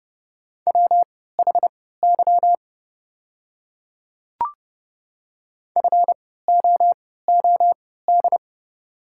Morse code, heard as a keyed tone, about 30 wpm, spaced as for 17 wpm, 700 Hz. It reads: WHY FOOD
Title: 3x Character Spacing - Top 300 Words - 30wpm